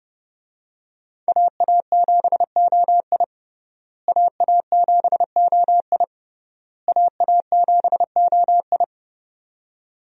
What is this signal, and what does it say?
Morse code, heard as a keyed tone, about 30 wpm, 700 Hz. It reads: AA7OI AA7OI AA7OI
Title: USA Call Signs - 5 Characters - Repeated 3x - 30wpm